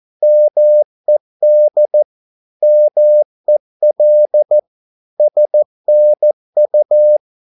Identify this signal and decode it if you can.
Morse code, heard as a keyed tone, about 14 wpm, 600 Hz. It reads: MED MEL SNU
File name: Korte ord DK 14wpm